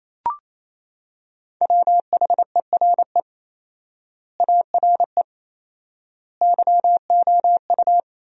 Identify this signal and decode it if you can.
Morse code, heard as a keyed tone, about 28 wpm, 700 Hz. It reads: WHERE ARE YOU
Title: 4x Word-Spacing - Sentences from Top 200 Words - 28wpm